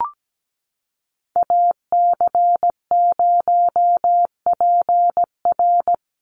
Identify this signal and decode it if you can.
Morse code, heard as a keyed tone, about 17 wpm, 700 Hz. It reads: AC0PR
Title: NA SOTA Activator Callsigns 17wpm